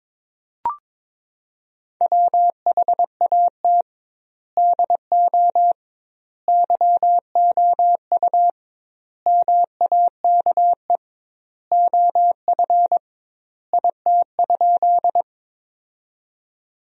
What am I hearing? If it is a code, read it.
Morse code, heard as a keyed tone, about 22 wpm, 700 Hz. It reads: WHAT DO YOU MAKE OF IT?
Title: 2x Word-Spacing - Sentences from Top 100 Words - 22wpm